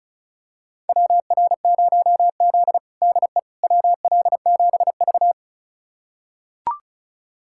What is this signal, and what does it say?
Morse code, heard as a keyed tone, about 36 wpm, 700 Hz. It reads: WR0Z DE WL7V
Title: USA Callsign DE Callsign - 4 Characters - 35wpm